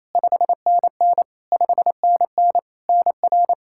Morse code, heard as a keyed tone, about 28 wpm, 700 Hz. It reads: 5NN 5NN NR
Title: Practice QSO - Exchange 1 - Beginning - 28wpm